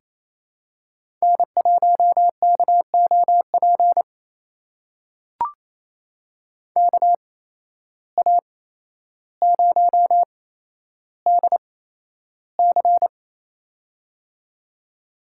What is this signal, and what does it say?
Morse code, heard as a keyed tone, about 28 wpm, spaced as for 12 wpm, 700 Hz. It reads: N1KOP KA0DC
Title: USA Call Signs - 5 Characters - 8x Character Spacing - 28wpm